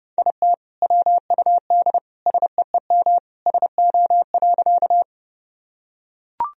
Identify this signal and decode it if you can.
Morse code, heard as a keyed tone, about 30 wpm, 700 Hz. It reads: IT WUD SEEM SO.
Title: Sentences from Top 300 Words and Abbreviations 30wpm